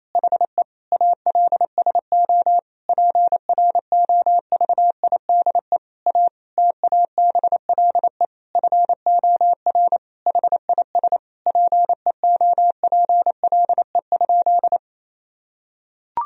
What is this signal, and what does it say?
Morse code, heard as a keyed tone, about 28 wpm, 700 Hz. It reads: HE ALSO PROVIDE A TABLE FOR HIS PEOPLE?
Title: Sentences from Top 1000 Words - Review Part 3 - 28wpm